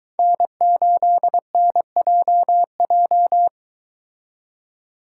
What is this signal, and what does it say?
Morse code, heard as a keyed tone, about 23 wpm, 700 Hz. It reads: N8NJJ